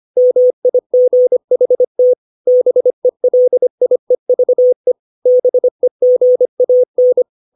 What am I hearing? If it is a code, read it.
Morse code, heard as a keyed tone, about 25 wpm, 500 Hz. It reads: MIGHT BELIEVE BEGAN
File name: 632 words / 632 Words 25WPM